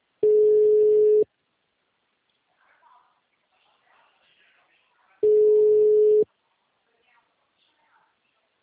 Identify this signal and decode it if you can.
Ringback tone (Europe, 425 Hz), with speech.